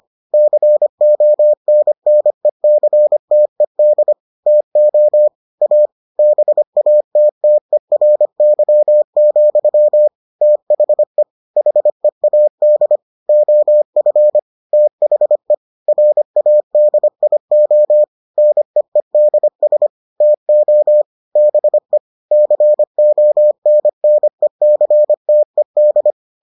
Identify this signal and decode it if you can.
Morse code, heard as a keyed tone, about 25 wpm, 600 Hz. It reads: CONNECTED TO A BATTERY, THE HEAD OF THE RADIO NEEDS TO BE CONNECTED